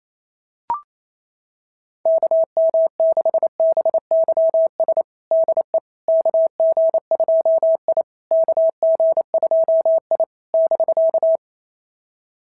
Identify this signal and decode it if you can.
Morse code, heard as a keyed tone, about 28 wpm, 690 Hz. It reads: KM6BYS DE KG2I KG2I <BK>
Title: SOTA QSO Practice 28wpm